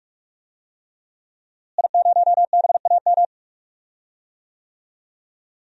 Morse code, heard as a keyed tone, about 46 wpm, 700 Hz. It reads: I0BAM
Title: DXCC-100 Call Signs - 5 Characters - Repeated 3x - 45wpm